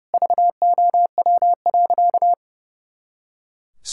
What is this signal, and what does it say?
Morse code, heard as a keyed tone, about 30 wpm, 700 Hz. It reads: VOW.
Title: Pangram Challenge v2 - 30wpm